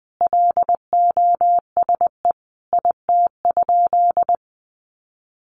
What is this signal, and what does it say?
Morse code, heard as a keyed tone, about 19 wpm, 700 Hz. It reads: LOSE IT?
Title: Sentences from Top 1100 Words 20wpm